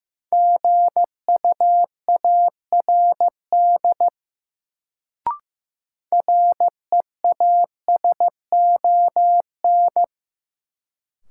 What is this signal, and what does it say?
Morse code, heard as a keyed tone, about 15 wpm, 700 Hz. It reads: GUARD REASON
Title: Sentences from Top 700 Words - New Word Drill - 15wpm